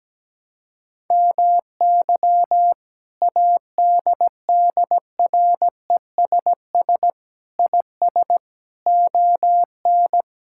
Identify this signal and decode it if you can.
Morse code, heard as a keyed tone, about 17 wpm, 700 Hz. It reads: MY ADDRESS IS ON
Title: Sentences from Top 1000 Words - Review Part 6 - 17wpm